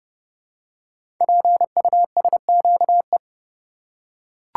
Morse code, heard as a keyed tone, about 30 wpm, 700 Hz. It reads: PUSQE
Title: Code Groups - 5 Letters - 30wpm